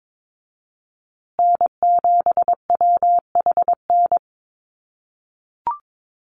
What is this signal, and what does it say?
Morse code, heard as a keyed tone, about 22 wpm, 700 Hz. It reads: N7WHN
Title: USA Call Signs Repeated 2x 22wpm